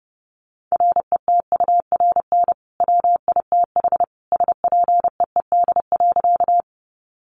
Morse code, heard as a keyed tone, about 30 wpm, 700 Hz. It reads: RETURN WITH SPEED.
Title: Sentences from Top 800 Words 30wpm